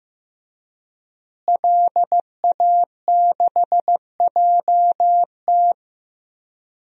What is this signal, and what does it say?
Morse code, heard as a keyed tone, about 15 wpm, 700 Hz. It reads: LA6JT
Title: DXCC-100 Call Signs - 5 Characters - Repeated 3x - 15wpm